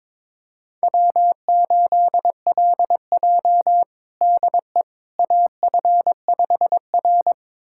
Morse code, heard as a keyed tone, about 22 wpm, 700 Hz. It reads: W8LJ DE AF5R
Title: USA Callsign DE Callsign - 4 Characters - 22wpm